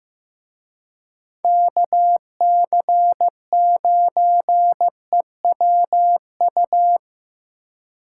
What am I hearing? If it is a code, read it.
Morse code, heard as a keyed tone, about 16 wpm, 700 Hz. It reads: KC9EWU
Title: Speed Racing - USA Call Signs 15wpm